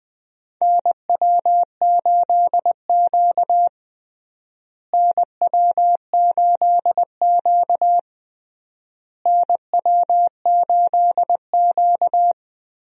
Morse code, heard as a keyed tone, about 21 wpm, 700 Hz. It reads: NW8Q NW8Q NW8Q
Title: USA Call Signs - 4 Characters - Repeated 3x - 20wpm